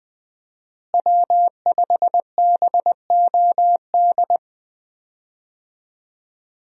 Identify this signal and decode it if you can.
Morse code, heard as a keyed tone, about 20 wpm, 700 Hz. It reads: W5BOD
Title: USA Call Signs - 5 Characters - Repeated 3x - 20wpm